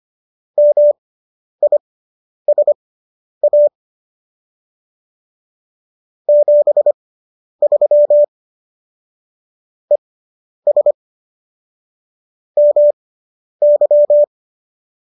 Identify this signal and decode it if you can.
Morse code, heard as a keyed tone, about 25 wpm, spaced as for 10 wpm, 590 Hz. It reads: MISA 73 ES MY